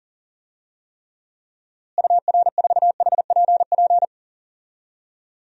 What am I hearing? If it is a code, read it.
Morse code, heard as a keyed tone, about 40 wpm, 700 Hz. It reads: UR4HPP
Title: DXCC-100 Call Signs - 6 Characters - Repeated 3x - 40wpm